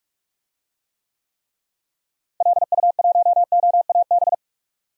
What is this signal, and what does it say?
Morse code, heard as a keyed tone, about 45 wpm, 700 Hz. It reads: LU1OAB
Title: DXCC-100 Call Signs - 6 Characters - Repeated 3x - 45wpm